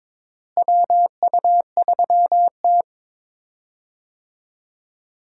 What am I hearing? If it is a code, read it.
Morse code, heard as a keyed tone, about 22 wpm, 700 Hz. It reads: WU3T
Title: USA Call Signs - 4 Characters - Repeated 2x - 22wpm